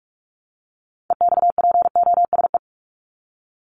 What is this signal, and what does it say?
Morse code, heard as a keyed tone, about 44 wpm, 700 Hz. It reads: EXPOSE
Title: Top 1501-1750 Words 45wpm